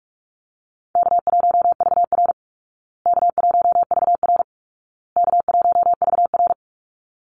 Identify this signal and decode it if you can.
Morse code, heard as a keyed tone, about 44 wpm, 700 Hz. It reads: K1VR K1VR K1VR
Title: Easy Call Signs Repeated 3x - 45wpm